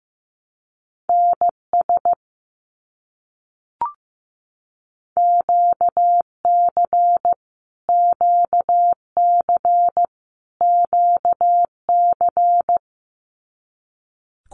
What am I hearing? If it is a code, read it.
Morse code, heard as a keyed tone, about 15 wpm, 700 Hz. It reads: NS QC QC QC
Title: Canadian Provinces and Territories - Repeated 3x - 15wpm